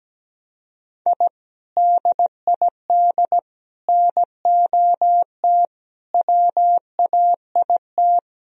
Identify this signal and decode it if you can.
Morse code, heard as a keyed tone, about 17 wpm, 700 Hz. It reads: I DID NOT WAIT
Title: Sentences from Top 1000 Words - Review Part 2 - 17wpm